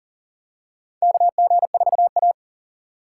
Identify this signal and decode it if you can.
Morse code, heard as a keyed tone, about 40 wpm, 700 Hz. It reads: KG4A
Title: USA Call Signs - 4 Characters - 2x Character Spacing - 40wpm